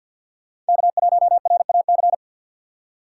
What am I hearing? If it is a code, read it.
Morse code, heard as a keyed tone, about 51 wpm, 700 Hz. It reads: K1RAC
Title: USA Call Signs - 5 Characters - 4x Character Spacing - 50wpm